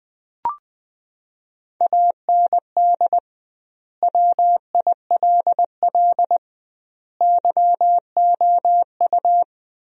Morse code, heard as a keyed tone, about 20 wpm, 700 Hz. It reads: AND WILL YOU
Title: 2x Word-Spacing - Sentences from Top 400 Words - 20wpm